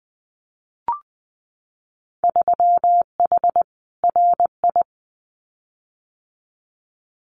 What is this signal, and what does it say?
Morse code, heard as a keyed tone, about 20 wpm, 700 Hz. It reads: 3H RI
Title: Winter Field Day Exchanges Encore 20wpm